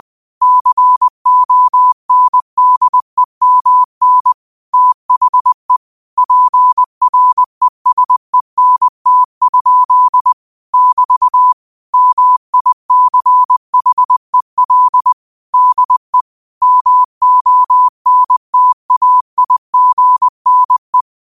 Morse code, heard as a keyed tone, about 20 wpm, 1000 Hz. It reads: CONDEMN THE PRESENT? <BT> MICHEL DE MONTAIGNE
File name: qotdfiles / 20wpm